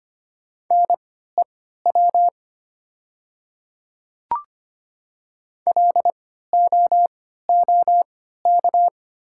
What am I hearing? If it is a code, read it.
Morse code, heard as a keyed tone, about 25 wpm, spaced as for 14 wpm, 700 Hz. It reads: NEW LOOK
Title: 3x Character Spacing - Top 300 Words - 25wpm